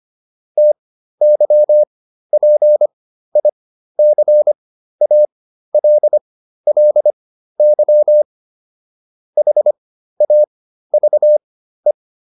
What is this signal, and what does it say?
Morse code, heard as a keyed tone, about 25 wpm, spaced as for 13 wpm, 600 Hz. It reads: TYPICALLY HAVE